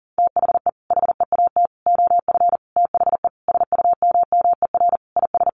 Morse code, heard as a keyed tone, about 40 wpm, 700 Hz. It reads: THE HEAT OF THE SUMMER IS
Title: Sentences from Top 1100 Words 40wpm